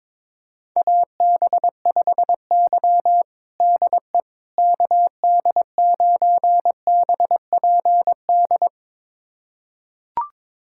Morse code, heard as a keyed tone, about 21 wpm, 700 Hz. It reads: AB5Y DE KD9BPD
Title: USA Callsign DE Callsign 22wpm